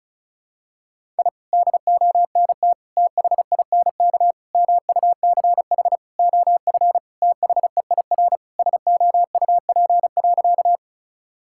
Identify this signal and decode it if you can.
Morse code, heard as a keyed tone, about 36 wpm, 700 Hz. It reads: I DONT THINK MUCH OF THEIR SOUP.
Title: Sentences from Top 1000 Words - Review Part 6 - 35wpm